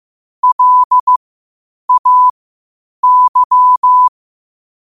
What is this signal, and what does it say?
Morse code, heard as a keyed tone, about 15 wpm, spaced as for 8 wpm, 1000 Hz. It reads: LAY
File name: qotdfiles / 8wpm